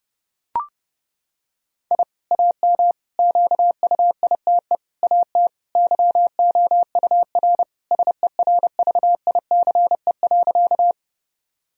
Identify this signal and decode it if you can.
Morse code, heard as a keyed tone, about 32 wpm, 700 Hz. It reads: I AM QUITE AT YOUR SERVICE.
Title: Sentences from Top 1000 Words - Review Part 4 - 30wpm